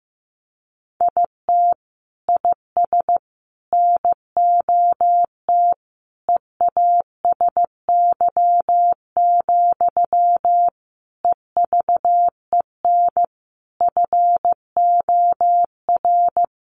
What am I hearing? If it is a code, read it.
Morse code, heard as a keyed tone, about 15 wpm, 700 Hz. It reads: IT IS NOT EASY, EVEN FOR
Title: Sentences from Top 700 Words 15wpm